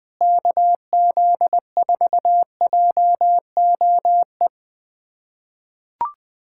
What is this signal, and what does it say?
Morse code, heard as a keyed tone, about 20 wpm, 700 Hz. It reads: KZ4JOE
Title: USA Callsign DE Callsign - 6 Characters - 20wpm